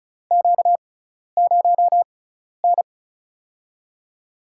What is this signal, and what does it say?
Morse code, heard as a keyed tone, about 36 wpm, spaced as for 12 wpm, 700 Hz. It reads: Q0N
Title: USA Call Signs - 6x Character Spacing - 35wpm